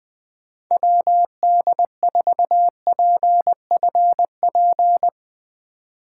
Morse code, heard as a keyed tone, about 20 wpm, 700 Hz. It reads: WD4PFP
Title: Speed Racing - USA 2x3 Call Signs 17wpm